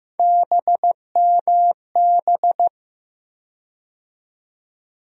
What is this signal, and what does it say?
Morse code, heard as a keyed tone, about 15 wpm, 700 Hz. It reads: BMB
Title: USA Call Sign Postfixes - 3 Letters 15wpm